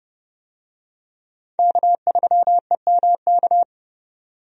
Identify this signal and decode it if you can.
Morse code, heard as a keyed tone, about 32 wpm, 700 Hz. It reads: K3EMK
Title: USA Call Signs - 5 Characters - 30wpm